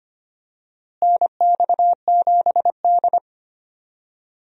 Morse code, heard as a keyed tone, about 25 wpm, 700 Hz. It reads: NX7D
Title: USA Call Signs - 4 Characters - Repeated 3x - 25wpm